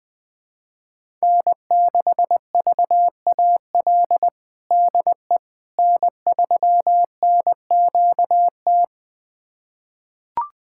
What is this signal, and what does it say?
Morse code, heard as a keyed tone, about 20 wpm, 700 Hz. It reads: N6VAL DE N3NQT
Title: USA Callsign DE Callsign - 5 Characters - 20wpm